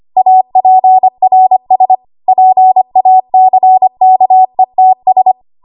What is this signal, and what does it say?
Morse code, heard as a keyed tone, about 25 wpm, 750 Hz. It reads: APRS PACKETS
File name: Morse / 25WPM